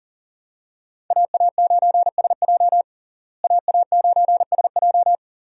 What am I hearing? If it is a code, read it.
Morse code, heard as a keyed tone, about 40 wpm, 700 Hz. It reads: AA9SJ AA9SJ
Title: USA Call Signs - 5 Characters - Repeated 2x - 40wpm